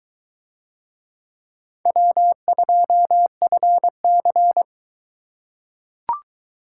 Morse code, heard as a keyed tone, about 21 wpm, 700 Hz. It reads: W2FC